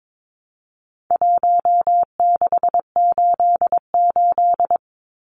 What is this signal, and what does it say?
Morse code, heard as a keyed tone, about 22 wpm, 700 Hz. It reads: 1688
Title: Numbers - Contest Style - 4-Digit 22wpm